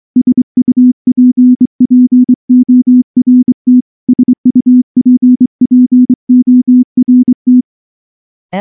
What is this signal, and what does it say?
Morse code, heard as a keyed tone, about 23 wpm, 250 Hz. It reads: SUPPORT SUPPORT
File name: morse / morse-training-25WPM